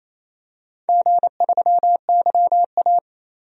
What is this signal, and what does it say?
Morse code, heard as a keyed tone, about 28 wpm, 700 Hz. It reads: G3YA
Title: DXCC-100 Call Signs - 4 Characters - 28wpm